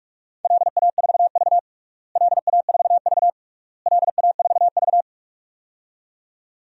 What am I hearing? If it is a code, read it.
Morse code, heard as a keyed tone, about 41 wpm, 700 Hz. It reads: LA4V LA4V LA4V
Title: DXCC-100 Call Signs - 4 Characters - Repeated 3x - 45wpm